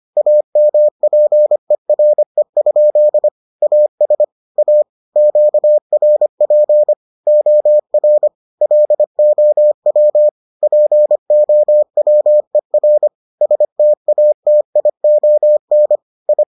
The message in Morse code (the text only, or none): AMPERE? AS A QRP OR LOW POWER STATION I